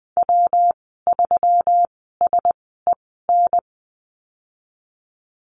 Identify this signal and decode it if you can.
Morse code, heard as a keyed tone, about 20 wpm, spaced as for 15 wpm, 700 Hz. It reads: W3SEN
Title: 154 Lesson 25g - Letter p - Call Signs 20wpm